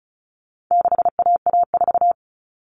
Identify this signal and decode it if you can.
Morse code, heard as a keyed tone, about 34 wpm, 700 Hz. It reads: 6AA4
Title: All Cut Numbers - 4-Digit 35wpm